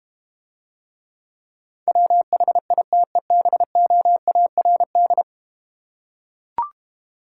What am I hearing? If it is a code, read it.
Morse code, heard as a keyed tone, about 32 wpm, 700 Hz. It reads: WHITEBOARD